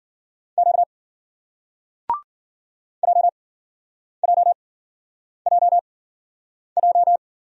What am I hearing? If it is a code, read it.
Morse code, heard as a keyed tone, about 56 wpm, 700 Hz. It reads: Q J J J J